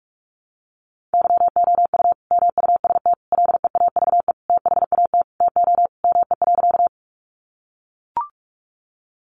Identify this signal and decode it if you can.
Morse code, heard as a keyed tone, about 44 wpm, 700 Hz. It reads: YOU MUST LEAVE THAT TO ME.
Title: Sentences from Top 400 Words 45wpm